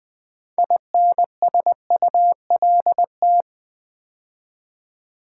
Morse code, heard as a keyed tone, about 20 wpm, 700 Hz. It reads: INSULT
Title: Build A Word - 3 Letters At A Time - 6-Letter Words - 20wpm